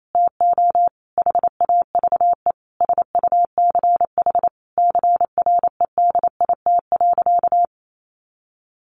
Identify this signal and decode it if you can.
Morse code, heard as a keyed tone, about 28 wpm, 700 Hz. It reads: TO HAVE SUCH CREDIT.
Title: Sentences from Top 1000 Words 28wpm